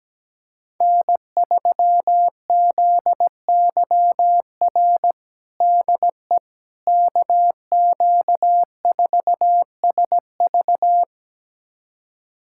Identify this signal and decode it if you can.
Morse code, heard as a keyed tone, about 17 wpm, 700 Hz. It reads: N3ZYR DE KQ4SV
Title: USA Callsign DE Callsign - 5 Characters - 17wpm